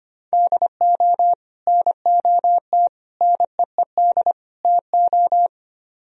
Morse code, heard as a keyed tone, about 29 wpm, 700 Hz. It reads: DO NOT NEED TO
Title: Speed-Racing - Sets of 4 Words 25wpm